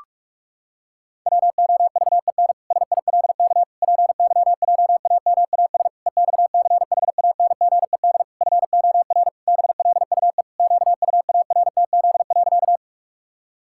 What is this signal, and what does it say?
Morse code, heard as a keyed tone, about 46 wpm, 700 Hz. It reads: WOVEN SILK PYJAMAS EXCHANGED FOR BLUE QUARTZ.
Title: Pangram Challenge v2 - 45wpm